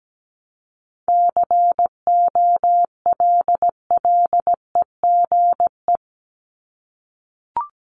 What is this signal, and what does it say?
Morse code, heard as a keyed tone, about 17 wpm, 700 Hz. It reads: COLLEGE
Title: Top 1001-1250 Words 17wpm